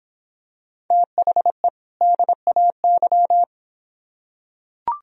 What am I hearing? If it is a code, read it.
Morse code, heard as a keyed tone, about 26 wpm, 700 Hz. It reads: THE DAY